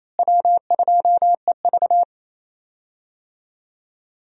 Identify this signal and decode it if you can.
Morse code, heard as a keyed tone, about 28 wpm, 700 Hz. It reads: W2EV
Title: USA Call Signs - 4 Characters - Repeated 2x - 28wpm